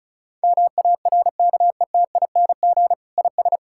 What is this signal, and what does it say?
Morse code, heard as a keyed tone, about 36 wpm, 700 Hz. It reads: MARKETING IS
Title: Sentences from Top 1000 Words - Review Part 5 - 35wpm